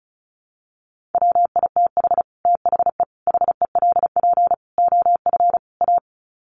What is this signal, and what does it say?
Morse code, heard as a keyed tone, about 34 wpm, 700 Hz. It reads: WITH THE HELP OF A
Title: Sets of 5 Words 35wpm